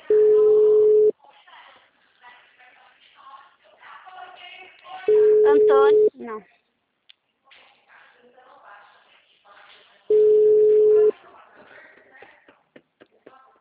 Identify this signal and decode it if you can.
Ringback tone (Europe, 425 Hz), with speech.